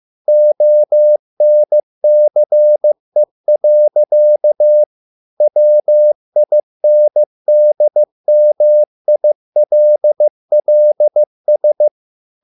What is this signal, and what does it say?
Morse code, heard as a keyed tone, about 15 wpm, 600 Hz. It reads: ONCE. WINDMILLS